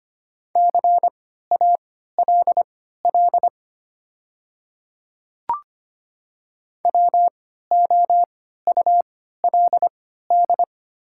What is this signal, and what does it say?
Morse code, heard as a keyed tone, about 25 wpm, spaced as for 14 wpm, 700 Hz. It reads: CALL WOULD
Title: 3x Character Spacing - Top 100 Words - 25wpm